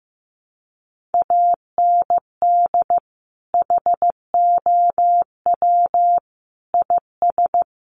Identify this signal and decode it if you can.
Morse code, heard as a keyed tone, about 15 wpm, 700 Hz. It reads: AND HOW IS
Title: Sentences from Top 1400 Words 15wpm